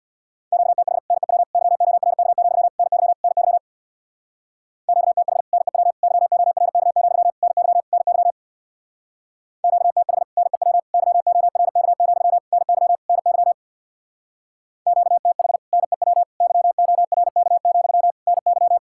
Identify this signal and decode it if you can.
Morse code, heard as a keyed tone, about 67 wpm, 700 Hz. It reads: QTH NEW YORK, NY NY QTH NEW YORK, NY NY QTH NEW YORK, NY NY QTH NEW YORK, NY